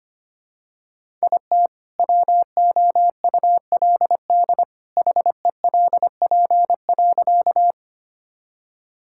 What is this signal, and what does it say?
Morse code, heard as a keyed tone, about 25 wpm, 700 Hz. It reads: IT WOULD HELP.
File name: practice / 25wpm / Sentences with 2-Word Phrases - 25wpm